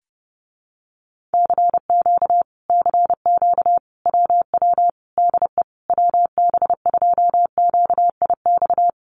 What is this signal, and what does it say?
Morse code, heard as a keyed tone, about 30 wpm, 700 Hz. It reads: CQ CQ WW DE WB2QIX
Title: Common QSO Phrases Encore 30wpm